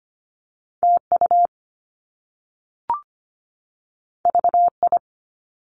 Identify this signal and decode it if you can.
Morse code, heard as a keyed tone, about 25 wpm, 700 Hz. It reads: TU VI
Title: Top Two Letter Combinations in Words 25wpm